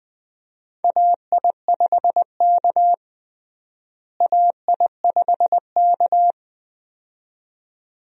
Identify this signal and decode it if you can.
Morse code, heard as a keyed tone, about 20 wpm, 700 Hz. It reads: AI5K AI5K
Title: USA Call Signs - 4 Characters - Repeated 2x - 20wpm